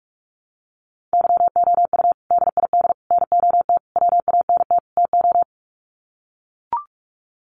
Morse code, heard as a keyed tone, about 44 wpm, 700 Hz. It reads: YOU DID NOT WANT TO
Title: Sets of 5 Words Encore2 45wpm